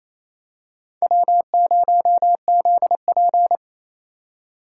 Morse code, heard as a keyed tone, about 28 wpm, 700 Hz. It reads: W0ZP
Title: K1USN SST QSO Practice - 28wpm